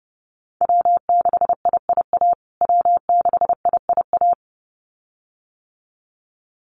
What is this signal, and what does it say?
Morse code, heard as a keyed tone, about 30 wpm, 700 Hz. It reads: W6IIA W6IIA
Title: USA Call Signs Repeated 2x 30wpm